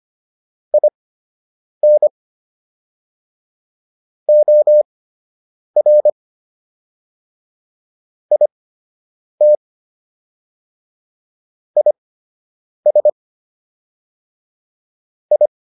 Morse code, heard as a keyed tone, about 25 wpm, spaced as for 8 wpm, 600 Hz. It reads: IN OR IT IS I